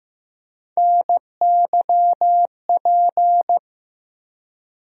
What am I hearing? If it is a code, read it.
Morse code, heard as a keyed tone, about 15 wpm, 700 Hz. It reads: NYP
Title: USA Call Sign Postfixes - 3 Letters 15wpm